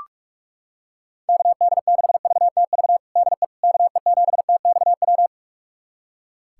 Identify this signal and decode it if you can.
Morse code, heard as a keyed tone, about 42 wpm, 700 Hz. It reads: KD6VTV DE KE7TXW
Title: USA Callsign DE Callsign 45wpm